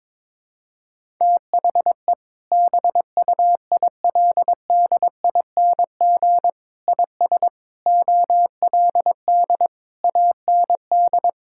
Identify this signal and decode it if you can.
Morse code, heard as a keyed tone, about 22 wpm, 700 Hz. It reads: THE BUILDING IS OLD AND
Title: Sentences from Top 1000 Words - Review Part 5 - 22wpm